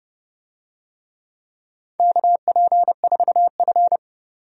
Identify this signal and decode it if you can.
Morse code, heard as a keyed tone, about 31 wpm, 700 Hz. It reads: KP4F
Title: DXCC-100 Call Signs - 4 Characters - 30wpm